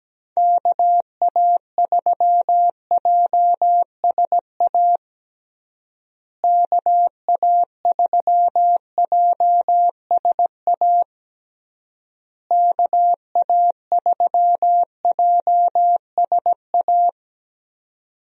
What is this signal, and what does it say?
Morse code, heard as a keyed tone, about 17 wpm, 700 Hz. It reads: KA3JSA KA3JSA KA3JSA
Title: USA Call Signs - 6 Characters - Repeated 3x - 17wpm